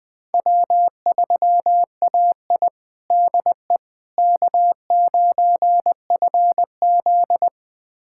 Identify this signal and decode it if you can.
Morse code, heard as a keyed tone, about 20 wpm, 700 Hz. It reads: W3AI DE K9FZ
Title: USA Callsign DE Callsign - 4 Characters - 20wpm